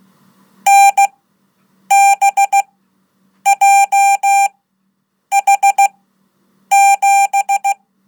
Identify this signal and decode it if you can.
Morse code, heard as a keyed tone, about 16 wpm, spaced as for 8 wpm, 790 Hz. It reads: NBJH7
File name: Code / 5min-1